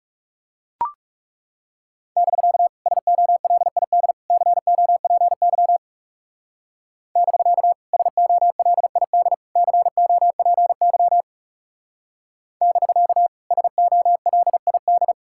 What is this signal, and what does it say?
Morse code, heard as a keyed tone, about 37 wpm, 700 Hz. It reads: <BK> SOLID COPY <BK> SOLID COPY <BK> SOLID
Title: Speed-Racing Common QSO Phrases - 30wpm